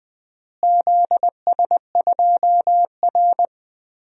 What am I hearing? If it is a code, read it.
Morse code, heard as a keyed tone, about 20 wpm, 700 Hz. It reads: ZS2R
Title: DXCC-100 Call Signs - 4 Characters - Repeated 3x - 20wpm